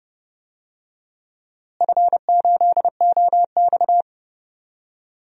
Morse code, heard as a keyed tone, about 30 wpm, 700 Hz. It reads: F8OX
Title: DXCC-100 Call Signs - 4 Characters - Repeated 2x - 30wpm